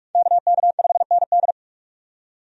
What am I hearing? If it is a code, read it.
Morse code, heard as a keyed tone, about 42 wpm, 700 Hz. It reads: KK5ND
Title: USA Call Signs - 5 Characters - 45wpm